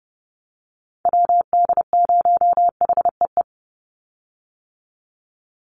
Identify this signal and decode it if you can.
Morse code, heard as a keyed tone, about 30 wpm, 700 Hz. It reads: WD0HEE
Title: USA Call Signs 30wpm